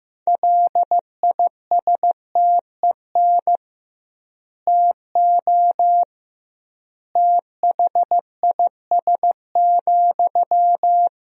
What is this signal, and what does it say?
Morse code, heard as a keyed tone, about 16 wpm, 700 Hz. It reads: LISTEN TO THIS,
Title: 2x Word-Spacing - Sentences from Top 400 Words - 15wpm